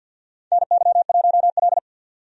Morse code, heard as a keyed tone, about 51 wpm, 700 Hz. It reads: NY1L
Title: USA Call Signs - 4 Characters - 50wpm